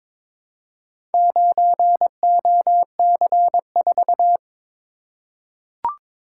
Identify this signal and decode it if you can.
Morse code, heard as a keyed tone, about 22 wpm, 700 Hz. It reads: 9OC4
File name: practice / 22wpm / Four Letter-Number Combinations - 22wpm